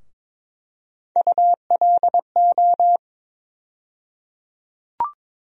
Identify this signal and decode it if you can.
Morse code, heard as a keyed tone, about 22 wpm, 700 Hz. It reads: ULO
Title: USA Call Sign Postfixes - 3 Letters 22wpm